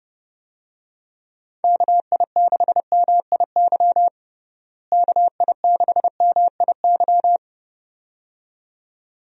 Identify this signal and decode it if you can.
Morse code, heard as a keyed tone, about 30 wpm, 700 Hz. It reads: KI6MIY KI6MIY
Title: USA Call Signs - 6 Characters - Repeated 2x - 30wpm